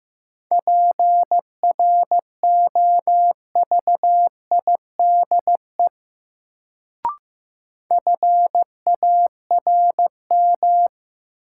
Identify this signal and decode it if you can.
Morse code, heard as a keyed tone, about 15 wpm, 700 Hz. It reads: PROVIDE FARM
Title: Sentences from Top 1000 Words - Review Part 2 - New Word Drill - 15wpm